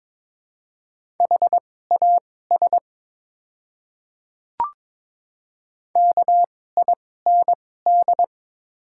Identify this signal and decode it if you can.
Morse code, heard as a keyed tone, about 22 wpm, spaced as for 16 wpm, 700 Hz. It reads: HAS KIND
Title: 2x Character Spacing - Top 200 Words - 22wpm